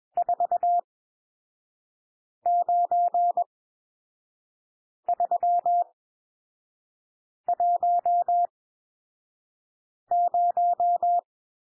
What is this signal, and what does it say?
Morse code, heard as a keyed tone, about 21 wpm, spaced as for 5 wpm, 700 Hz. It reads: 49310
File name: cwmp3 / N1 SOUND NUMBERS 21 wpm